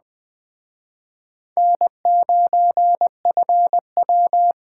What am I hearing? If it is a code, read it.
Morse code, heard as a keyed tone, about 20 wpm, 700 Hz. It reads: N9FW